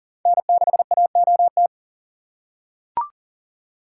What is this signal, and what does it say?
Morse code, heard as a keyed tone, about 40 wpm, 700 Hz. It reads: N6AOT
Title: USA Call Signs - 5 Characters - 3x Character Spacing - 40wpm